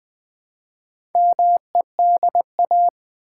Morse code, heard as a keyed tone, about 20 wpm, 700 Hz. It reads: MEDA
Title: Build A Word - 1 Letter At A Time - 3 to 5-Letter Words - 20wpm